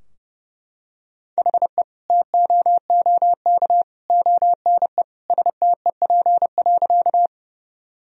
Morse code, heard as a keyed tone, about 30 wpm, 700 Hz. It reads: HE TOOK ONE STEP.
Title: Sentences from Top 1000 Words - Review Part 2 - 30wpm